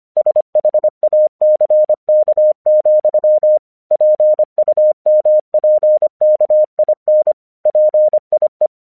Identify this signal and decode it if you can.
Morse code, heard as a keyed tone, about 25 wpm, 610 Hz. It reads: SHACK, PUMPKIN PIE
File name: 1 English 25wpm / AQC23